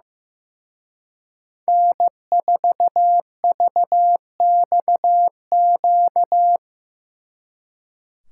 Morse code, heard as a keyed tone, about 16 wpm, 700 Hz. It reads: N4VXQ
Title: Speed Racing - USA 1x3 Call Signs 15wpm